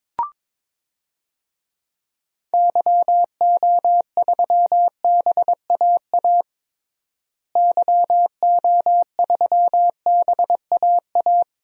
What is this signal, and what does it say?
Morse code, heard as a keyed tone, about 22 wpm, 700 Hz. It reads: YO3BAA YO3BAA
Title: DXCC-100 Call Signs - 6 Characters - Repeated 3x - 22wpm